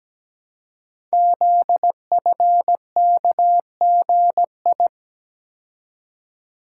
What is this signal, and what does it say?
Morse code, heard as a keyed tone, about 17 wpm, 700 Hz. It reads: ZFKGI
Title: Code Groups - 5 Letters - 17wpm